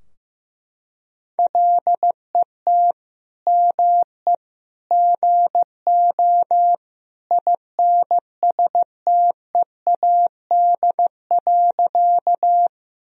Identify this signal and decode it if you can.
Morse code, heard as a keyed tone, about 15 wpm, 700 Hz. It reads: LET ME GO INSTEAD.
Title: Sentences from Top 1000 Words - Review Part 4 - 15wpm